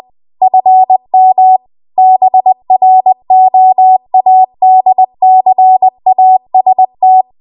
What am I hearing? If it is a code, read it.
Morse code, heard as a keyed tone, about 20 wpm, 750 Hz. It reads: FM BROADCAST
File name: Archive / 20 WPM / 20WPM